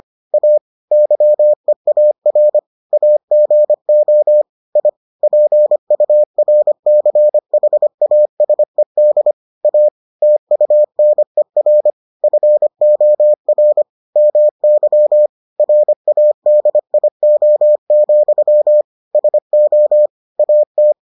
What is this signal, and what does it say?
Morse code, heard as a keyed tone, about 25 wpm, 600 Hz. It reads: A YEAR AGO I PURCHASED A TUNER FOR MY RADIO, SO AT